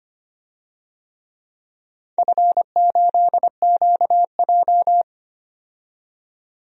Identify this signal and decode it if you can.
Morse code, heard as a keyed tone, about 25 wpm, 700 Hz. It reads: F8QJ
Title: DXCC-100 Call Signs - 4 Characters - 25wpm